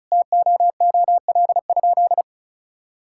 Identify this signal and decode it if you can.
Morse code, heard as a keyed tone, about 36 wpm, 700 Hz. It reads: TOOL?
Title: Sentences from Top 1000 Words - Review Part 6 - 35wpm